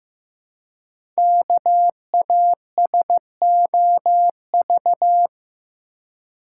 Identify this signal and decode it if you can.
Morse code, heard as a keyed tone, about 15 wpm, 700 Hz. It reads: KASOV
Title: Code Groups - 5 Letters - 15wpm